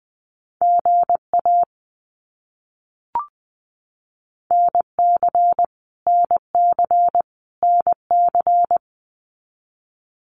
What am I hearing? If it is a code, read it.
Morse code, heard as a keyed tone, about 20 wpm, 700 Hz. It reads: GA NC NC NC
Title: US States Repeated 3x 20wpm